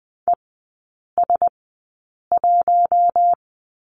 Morse code, heard as a keyed tone, about 20 wpm, spaced as for 8 wpm, 700 Hz. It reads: ES1
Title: 041 Lesson 8e - Number 1 - Three Characters 20wpm